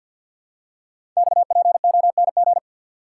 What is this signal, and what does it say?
Morse code, heard as a keyed tone, about 51 wpm, 700 Hz. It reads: XPONG
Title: Code Groups - 5 Letters - 50wpm